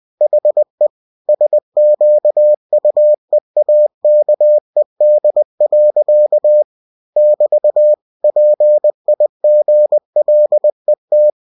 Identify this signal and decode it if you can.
Morse code, heard as a keyed tone, about 20 wpm, 600 Hz. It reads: HE SQUEAKED. <BT> PIGLET
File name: cw-books / winnie-the-pooh / winnie-20wpm-page-0106